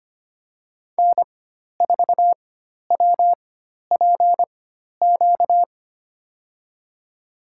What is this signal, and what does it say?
Morse code, heard as a keyed tone, about 25 wpm, spaced as for 12 wpm, 700 Hz. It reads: N4WPQ
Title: USA Call Signs - 5 Characters - 4x Character Spacing - 25wpm